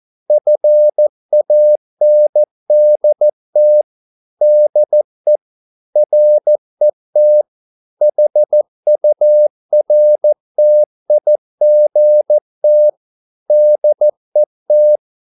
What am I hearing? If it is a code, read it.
Morse code, heard as a keyed tone, about 14 wpm, 600 Hz. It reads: FANDT DE RET HURTIGT DET
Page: Ferie 13 wpm